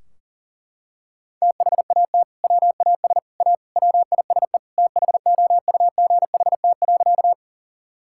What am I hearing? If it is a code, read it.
Morse code, heard as a keyed tone, about 40 wpm, 700 Hz. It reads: THAT WAS A WISE THOUGHT.
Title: Sentences from Top 1600 Words 40wpm